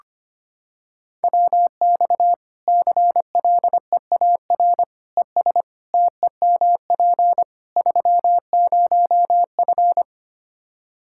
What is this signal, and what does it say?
Morse code, heard as a keyed tone, about 25 wpm, 700 Hz. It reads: WX CLEAR ES TEMP 30F
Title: Practice QSO - Exchange 2 - Info - 25wpm